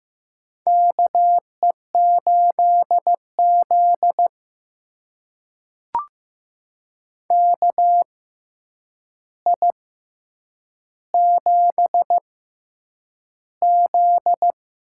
Morse code, heard as a keyed tone, about 15 wpm, spaced as for 8 wpm, 700 Hz. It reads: KE8Z KI7Z
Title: USA Call Signs - 6x Character Spacing - 15wpm